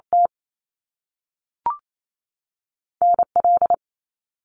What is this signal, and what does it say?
Morse code, heard as a keyed tone, about 28 wpm, 700 Hz. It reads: T N L
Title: Top Two Letter Combinations in Words 28wpm